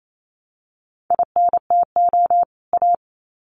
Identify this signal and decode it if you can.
Morse code, heard as a keyed tone, about 28 wpm, 700 Hz. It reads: INTO A
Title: Sets of 2 Words 28wpm